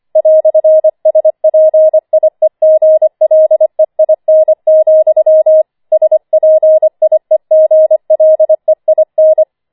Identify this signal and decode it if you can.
Morse code, heard as a keyed tone, about 25 wpm, 630 Hz. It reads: "SPIEGLEIN, SPIEGLEIN
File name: 25wpm